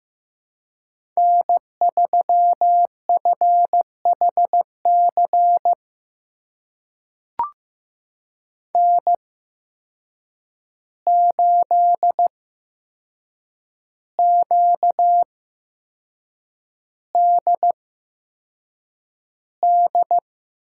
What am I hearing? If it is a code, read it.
Morse code, heard as a keyed tone, about 15 wpm, spaced as for 6 wpm, 700 Hz. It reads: N3FHC N8QDD
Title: USA Call Signs - 8x Character Spacing - 15wpm